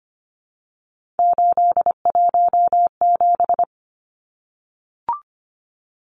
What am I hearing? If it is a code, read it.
Morse code, heard as a keyed tone, about 25 wpm, 700 Hz. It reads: 817
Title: Numbers - 3-Digit 25wpm